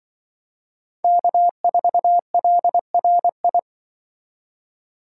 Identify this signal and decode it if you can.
Morse code, heard as a keyed tone, about 24 wpm, 700 Hz. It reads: K4LRI
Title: Speed Racing - USA Call Signs 20wpm